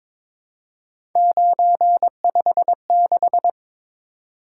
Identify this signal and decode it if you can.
Morse code, heard as a keyed tone, about 22 wpm, 700 Hz. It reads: 956
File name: practice / 22wpm / Three Letter-Number Combinations - 22wpm